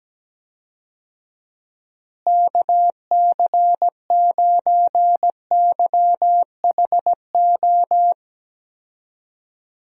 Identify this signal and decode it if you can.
Morse code, heard as a keyed tone, about 17 wpm, 700 Hz. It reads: KC9YHO
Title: USA Call Signs - 6 Characters - Repeated 2x - 17wpm